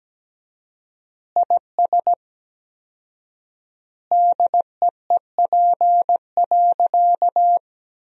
Morse code, heard as a keyed tone, about 17 wpm, 700 Hz. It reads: IS DEEP.
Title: 4x Word-Spacing - Sentences from Top 500 Words - 17wpm